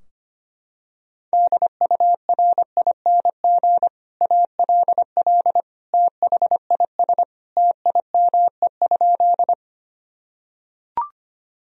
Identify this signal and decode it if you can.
Morse code, heard as a keyed tone, about 25 wpm, 700 Hz. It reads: DURING ALL THIS TIME?
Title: Sentences from Top 1000 Words - Review Part 2 - 25wpm